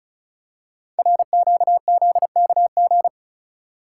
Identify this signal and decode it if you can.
Morse code, heard as a keyed tone, about 36 wpm, 700 Hz. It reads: RQZKG
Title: Code Groups - 5 Letter:Numbers - 35wpm